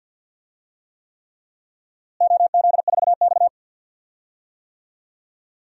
Morse code, heard as a keyed tone, about 50 wpm, 700 Hz. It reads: OZ4X
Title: DXCC-100 Call Signs - 4 Characters - Repeated 3x - 50wpm